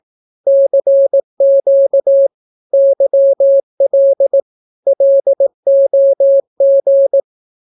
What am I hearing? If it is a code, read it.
Morse code, heard as a keyed tone, about 18 wpm, 550 Hz. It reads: CQ YL LOG
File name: naukapodstaw / skroty / skroty